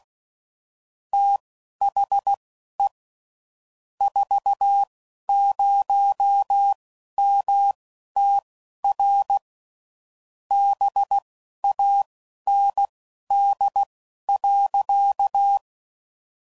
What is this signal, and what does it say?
Morse code, heard as a keyed tone, about 16 wpm, spaced as for 11 wpm, 800 Hz. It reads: THE 40MTR BAND.
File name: CW Code Groups @10WPM / QSOs / QSO kc7haa